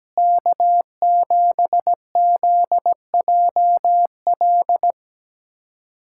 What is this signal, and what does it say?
Morse code, heard as a keyed tone, about 17 wpm, 700 Hz. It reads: K7ZJL
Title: USA Callsign DE Callsign - 5 Characters - 17wpm